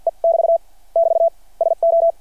Morse code, heard as a keyed tone, about 52 wpm, spaced as for 45 wpm, 690 Hz. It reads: E <BT> <BT> SO
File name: archived / CRSnet